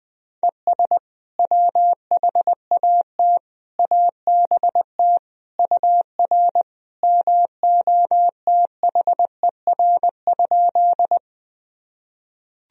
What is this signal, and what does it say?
Morse code, heard as a keyed tone, about 21 wpm, 700 Hz. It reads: ES WHAT ABT UR MOTHER?
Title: Sentences from Top 200 Words and Abbreviations 20wpm